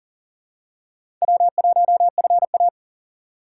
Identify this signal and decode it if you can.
Morse code, heard as a keyed tone, about 40 wpm, 700 Hz. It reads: W1FA
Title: USA Call Signs - 4 Characters - 4x Character Spacing - 40wpm